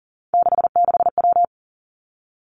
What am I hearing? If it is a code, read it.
Morse code, heard as a keyed tone, about 38 wpm, 700 Hz. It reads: 66W
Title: Three Challenging Letter-Number Combinations 40wpm